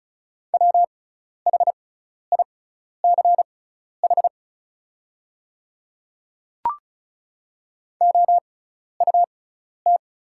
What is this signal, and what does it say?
Morse code, heard as a keyed tone, about 36 wpm, spaced as for 12 wpm, 690 Hz. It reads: WHICH OUT
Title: 6x Character Spacing - Top 300 Words - 35wpm